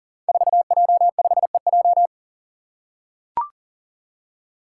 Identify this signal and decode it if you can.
Morse code, heard as a keyed tone, about 40 wpm, 700 Hz. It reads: 4J5EJ
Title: Five Challenging Letter-Number Combinations 40wpm